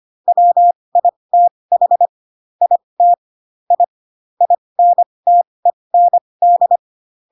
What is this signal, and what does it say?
Morse code, heard as a keyed tone, about 25 wpm, spaced as for 20 wpm, 700 Hz. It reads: WITH IT I INTEND
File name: Pinocchio / Pinocchio ch2 15 25 ik1hge